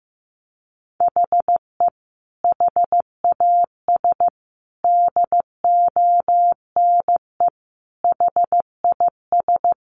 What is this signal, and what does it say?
Morse code, heard as a keyed tone, about 15 wpm, 700 Hz. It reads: HE HAS DONE HIS
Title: Sentences from Top 1500 Words 15wpm